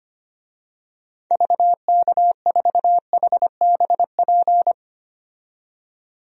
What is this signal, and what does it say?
Morse code, heard as a keyed tone, about 25 wpm, 690 Hz. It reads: VK4HBP
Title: DXCC-100 Call Signs - 6 Characters - 25wpm